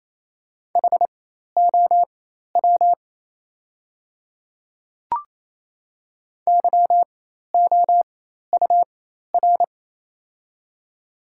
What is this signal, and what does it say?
Morse code, heard as a keyed tone, about 28 wpm, spaced as for 13 wpm, 700 Hz. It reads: HOW YOUR
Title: 4x Character Spacing - Top 100 Words - 28wpm